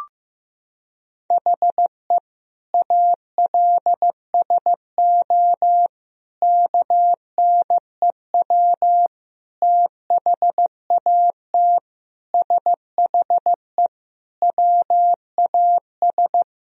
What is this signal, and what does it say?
Morse code, heard as a keyed tone, about 15 wpm, 700 Hz. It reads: HE ALSO KNEW THAT SHE WAS
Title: Sentences from Top 1000 Words - Review Part 1 - 15wpm